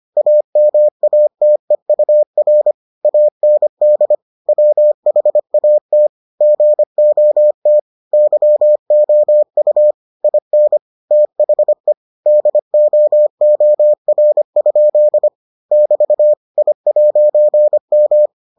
Morse code, heard as a keyed tone, about 25 wpm, 600 Hz. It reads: AMATEUR AND WHAT GOT YOU IN THE DOOR? <BT> I'M